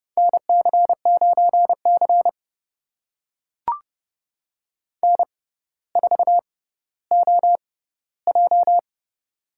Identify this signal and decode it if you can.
Morse code, heard as a keyed tone, about 30 wpm, spaced as for 15 wpm, 700 Hz. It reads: NC9C N4OJ
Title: USA Call Signs - 4 Characters - 6x Character Spacing - 30wpm